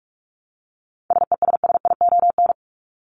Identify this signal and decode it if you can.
Morse code, heard as a keyed tone, about 44 wpm, 700 Hz. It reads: SESSION
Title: Top 1501-1750 Words 45wpm